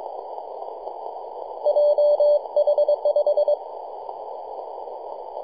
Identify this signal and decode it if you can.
Morse code, heard as a keyed tone, about 23 wpm, 590 Hz. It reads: JH5
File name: 3d2ya-18-085-cw